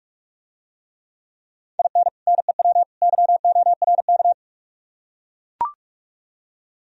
Morse code, heard as a keyed tone, about 44 wpm, 700 Hz. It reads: IN NEW YORK